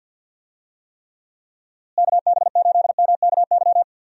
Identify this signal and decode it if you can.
Morse code, heard as a keyed tone, about 51 wpm, 700 Hz. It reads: KB8MKY
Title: USA Call Signs - 6 Characters - Repeated 3x - 50wpm